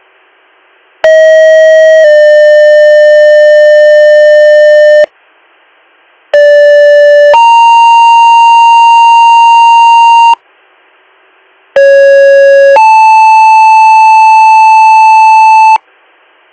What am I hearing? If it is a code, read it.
Two-tone page, heard as a one-second tone, then a three-second tone, about 640 then 600 Hz, about 580 then 930 Hz, about 550 then 880 Hz, with noise. It is heard from 1.0 to 15.8 s.